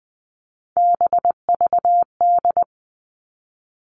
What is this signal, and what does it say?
Morse code, heard as a keyed tone, about 20 wpm, 700 Hz. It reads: BVD
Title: Three Challenging Letter-Number Combinations 20wpm